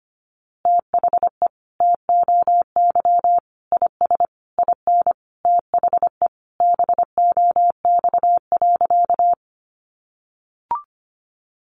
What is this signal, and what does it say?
Morse code, heard as a keyed tone, about 25 wpm, 700 Hz. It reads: THE TOY IS IN THE BOX.
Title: Easy Phrases 25wpm